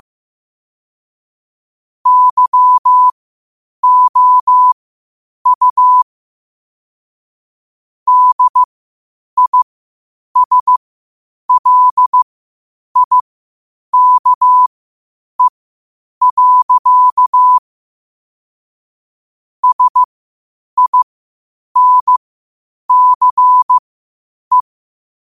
Morse code, heard as a keyed tone, about 15 wpm, spaced as for 8 wpm, 1000 Hz. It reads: YOU DISLIKE. SINCE
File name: qotdfiles / 8wpm